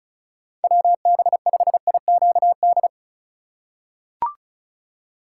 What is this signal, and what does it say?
Morse code, heard as a keyed tone, about 36 wpm, 700 Hz. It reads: WB5IQD
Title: USA Call Signs - 4x Character Spacing - 35wpm